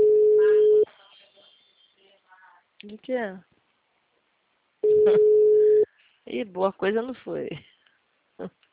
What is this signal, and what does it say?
Ringback tone (Europe, 425 Hz), with speech.